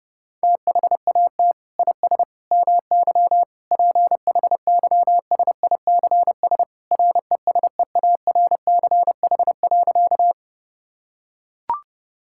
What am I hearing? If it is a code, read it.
Morse code, heard as a keyed tone, about 30 wpm, 700 Hz. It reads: THAT IS MY PHYSICS RESEARCH.
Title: Sentences from Top 1000 Words - Review Part 5 - 30wpm